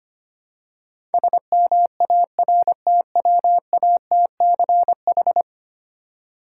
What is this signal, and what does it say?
Morse code, heard as a keyed tone, about 25 wpm, 700 Hz. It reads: SMARTWATCH
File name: practice / 25wpm / Difficult Words with Hints - 25wpm